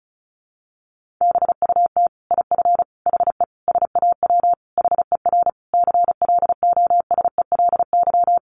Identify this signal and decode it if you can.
Morse code, heard as a keyed tone, about 34 wpm, 700 Hz. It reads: BUT IF HE SAW HER CLOSELY
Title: Sentences from Top 1400 Words 35wpm